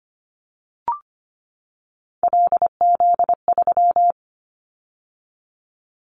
Morse code, heard as a keyed tone, about 25 wpm, 700 Hz. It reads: LZ3
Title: Callsign Prefixes for Top 100 DXCC - 25wpm